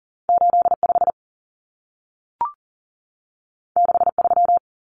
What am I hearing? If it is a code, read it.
Morse code, heard as a keyed tone, about 40 wpm, 700 Hz. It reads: 85 63
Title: Numbers - 2-Digit 40wpm